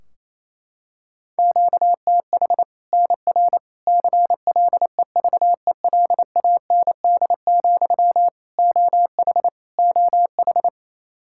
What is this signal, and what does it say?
Morse code, heard as a keyed tone, about 28 wpm, 700 Hz. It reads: QTH NR CLEVELAND, OH OH
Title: Practice QSO - Exchange 1 - Beginning - 28wpm